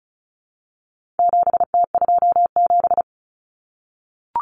Morse code, heard as a keyed tone, about 34 wpm, 700 Hz. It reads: 7T27
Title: Common Cut Numbers - 4-Digit 35wpm